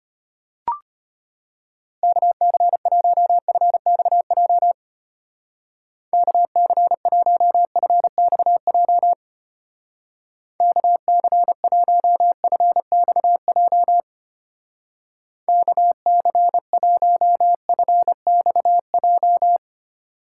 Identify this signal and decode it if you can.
Morse code, heard as a keyed tone, about 29 wpm, 700 Hz. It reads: KC1FXJ KC1FXJ KC1FXJ KC1FXJ
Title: Speed Racing - USA 2x3 Call Signs 25wpm